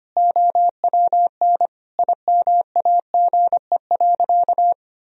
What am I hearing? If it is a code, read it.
Morse code, heard as a keyed tone, about 25 wpm, 700 Hz. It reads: OWN IMAGE.
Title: Sentences from Top 1000 Words - Review Part 4 - 25wpm